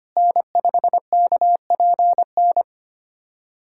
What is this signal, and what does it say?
Morse code, heard as a keyed tone, about 25 wpm, 700 Hz. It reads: N5KPN
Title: USA Callsign DE Callsign - 5 Characters - 25wpm